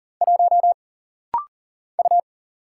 Morse code, heard as a keyed tone, about 40 wpm, 700 Hz. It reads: 1 U
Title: Single Letter-Number Alt - Rapid-Fire 40wpm